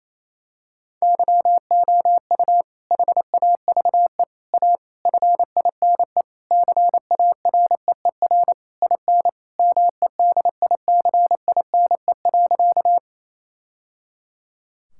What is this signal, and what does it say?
Morse code, heard as a keyed tone, about 28 wpm, 700 Hz. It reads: YOU HAVE A FINE CAREER IN MEDICINE.
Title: Sentences from Top 1000 Words - Review Part 6 - 28wpm